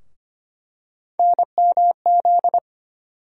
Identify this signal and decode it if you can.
Morse code, heard as a keyed tone, about 25 wpm, 700 Hz. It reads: NMZ
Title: USA Call Sign Postfixes - 3 Letters 25wpm